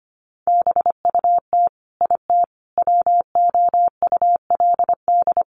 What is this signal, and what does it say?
Morse code, heard as a keyed tone, about 25 wpm, 700 Hz. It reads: BUT IT WOULD
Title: Sentences from Top 1500 Words 25wpm